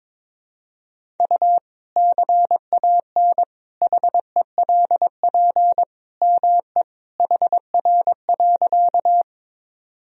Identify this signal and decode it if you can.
Morse code, heard as a keyed tone, about 22 wpm, 700 Hz. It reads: U CAN HELP ME HR.
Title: Sentences from Top 200 Words and Abbreviations 22wpm